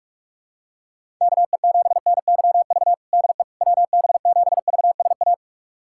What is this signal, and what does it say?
Morse code, heard as a keyed tone, about 46 wpm, 700 Hz. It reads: KE7NYV DE WB7VSA
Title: USA Callsign DE Callsign 45wpm